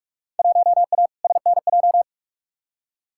Morse code, heard as a keyed tone, about 47 wpm, 700 Hz. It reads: 1A SNJ
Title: Field Day Exchanges - 45wpm